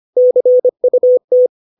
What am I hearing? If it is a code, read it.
Morse code, heard as a keyed tone, about 25 wpm, 500 Hz. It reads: CUT
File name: 632 words / 632 Words 25WPM